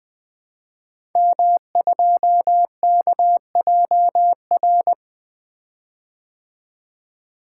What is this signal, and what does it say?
Morse code, heard as a keyed tone, about 20 wpm, 700 Hz. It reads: M2KJR
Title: Code Groups - 5 Letter:Numbers - 20wpm